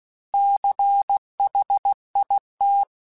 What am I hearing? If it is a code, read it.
Morse code, heard as a keyed tone, about 16 wpm, 800 Hz. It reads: CHIT
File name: Alphabetical / Alphabetical mp3s / Cs @15